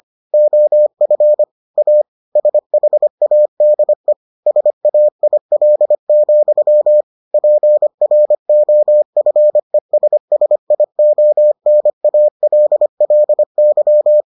Morse code, heard as a keyed tone, about 25 wpm, 600 Hz. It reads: OF A SHADE SAIL, PROFESSIONALLY